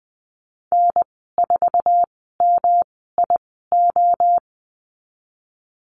Morse code, heard as a keyed tone, about 20 wpm, spaced as for 15 wpm, 700 Hz. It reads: N4MIO
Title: 167 Lesson 27g - Number 7 - Call Signs 20wpm